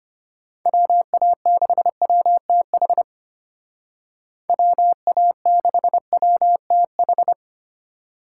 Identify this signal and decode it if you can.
Morse code, heard as a keyed tone, about 31 wpm, 700 Hz. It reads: WA6WTH WA6WTH
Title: Speed Racing - USA Call Signs 25wpm